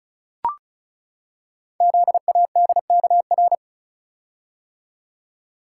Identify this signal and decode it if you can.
Morse code, heard as a keyed tone, about 36 wpm, 700 Hz. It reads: ZADKR
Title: Code Groups - 5 Letters - 35wpm